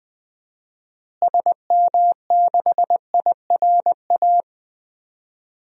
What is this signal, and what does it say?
Morse code, heard as a keyed tone, about 20 wpm, 700 Hz. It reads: SM6IRA
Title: DXCC-100 Call Signs - 6 Characters - Repeated 3x - 20wpm